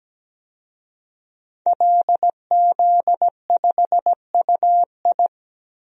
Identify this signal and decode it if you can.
Morse code, heard as a keyed tone, about 17 wpm, 700 Hz. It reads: LZ5UI
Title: DXCC-100 Call Signs - 5 Characters - 17wpm